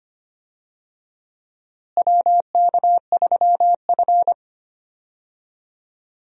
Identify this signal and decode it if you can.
Morse code, heard as a keyed tone, about 25 wpm, 700 Hz. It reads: WK3F
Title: USA Call Signs - 4 Characters - Repeated 2x - 25wpm